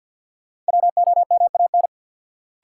Morse code, heard as a keyed tone, about 53 wpm, 700 Hz. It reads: WOMAN
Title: Sentences from Top 1000 Words - Review Part 2 - New Word Drill - 50wpm